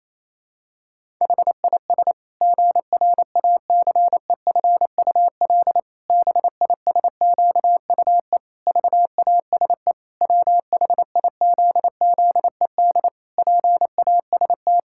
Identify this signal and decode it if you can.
Morse code, heard as a keyed tone, about 28 wpm, 700 Hz. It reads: HIS GRACEFUL BISQUE VASE WHIZZED PAST
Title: Pangram Challenge v2 - 28wpm